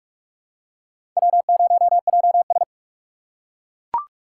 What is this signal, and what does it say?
Morse code, heard as a keyed tone, about 42 wpm, 700 Hz. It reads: W0JS